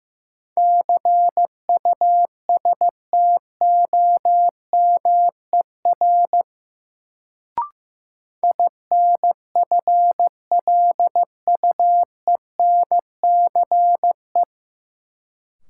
Morse code, heard as a keyed tone, about 15 wpm, 700 Hz. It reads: CUSTOMER INFLUENCE
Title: Sentences from Top 1100 Words - New Word Drill - 15wpm